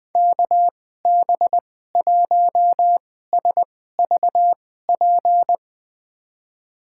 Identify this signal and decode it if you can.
Morse code, heard as a keyed tone, about 20 wpm, spaced as for 14 wpm, 700 Hz. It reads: KB1SVP
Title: USA Call Signs - 6 Characters - 2x Character Spacing - 20wpm